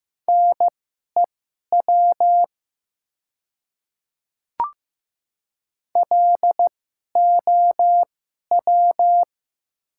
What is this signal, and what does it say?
Morse code, heard as a keyed tone, about 15 wpm, spaced as for 11 wpm, 700 Hz. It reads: NEW LOW
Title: 2x Character Spacing - Top 500 Words - 15wpm